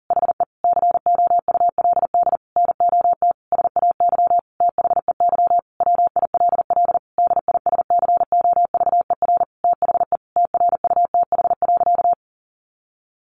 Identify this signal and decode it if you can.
Morse code, heard as a keyed tone, about 40 wpm, 700 Hz. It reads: HE COULD NOT SAY THEY WILL DISCOVER THE TRUTH.
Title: Sentences from Top 1100 Words 40wpm